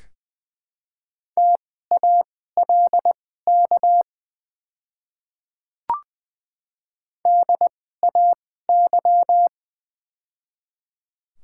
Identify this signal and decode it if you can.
Morse code, heard as a keyed tone, about 20 wpm, spaced as for 14 wpm, 690 Hz. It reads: TALK DAY
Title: 2x Character Spacing - Top 500 Words - 20wpm